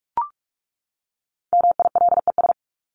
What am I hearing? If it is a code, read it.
Morse code, heard as a keyed tone, about 44 wpm, 700 Hz. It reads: MILES
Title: ES:S Suffix - Top 100 Words with Base Words 45wpm